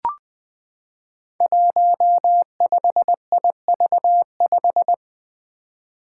Morse code, heard as a keyed tone, about 21 wpm, 700 Hz. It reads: 15IV5
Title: Five Challenging Letter-Number Combinations 20wpm